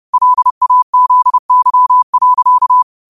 Morse code, heard as a keyed tone, about 30 wpm, 1000 Hz. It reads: LAZY.
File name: qotdfiles / 30wpm